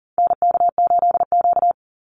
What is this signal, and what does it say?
Morse code, heard as a keyed tone, about 40 wpm, 700 Hz. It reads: NK8Q
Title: NA SOTA Activator Callsigns 40wpm